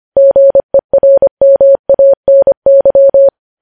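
Morse code, heard as a keyed tone, about 25 wpm, 550 Hz. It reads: GERMANY
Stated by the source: Morse code practice of news headlines sent at 25 wpm.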